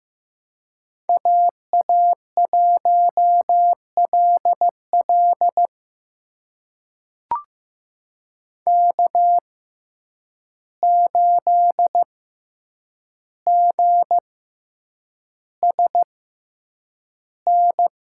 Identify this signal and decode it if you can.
Morse code, heard as a keyed tone, about 15 wpm, spaced as for 8 wpm, 700 Hz. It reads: AA1LL K8GSN
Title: USA Call Signs - 5 Characters - 6x Character Spacing - 15wpm